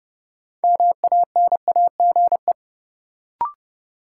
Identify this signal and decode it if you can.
Morse code, heard as a keyed tone, about 32 wpm, 700 Hz. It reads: MANAGE
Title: Sentences from Top 1400 Words - New Word Drill - 30wpm